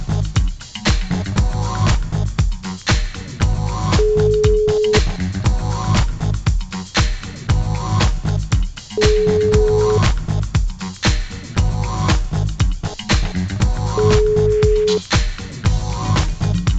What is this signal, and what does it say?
Ringback tone (Europe, 425 Hz), with noise.